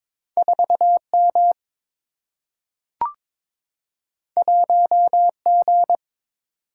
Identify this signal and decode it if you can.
Morse code, heard as a keyed tone, about 22 wpm, 700 Hz. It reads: 4 M 1 G
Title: Single-Letter Single-Number 22wpm